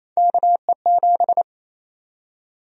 Morse code, heard as a keyed tone, about 28 wpm, 700 Hz. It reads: KE7
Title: USA Call Sign Prefixes - 28wpm